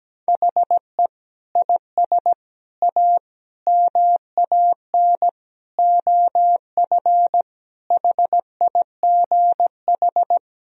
Morse code, heard as a keyed tone, about 17 wpm, 700 Hz. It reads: HE IS A MAN OF HIGH
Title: Sentences from Top 1000 Words - Review Part 2 - 17wpm